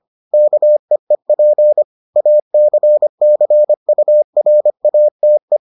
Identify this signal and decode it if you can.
Morse code, heard as a keyed tone, about 25 wpm, 600 Hz. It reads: KEEP ACCURATE